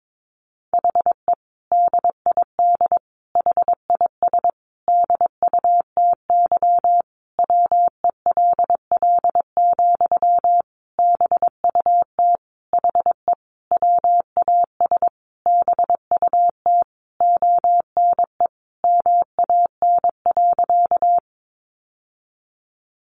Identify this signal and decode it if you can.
Morse code, heard as a keyed tone, about 22 wpm, 700 Hz. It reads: HE DID HIS DUTY WELL, BUT HE WAS BUT ONE MAN.
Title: Sentences from Top 1500 Words 22wpm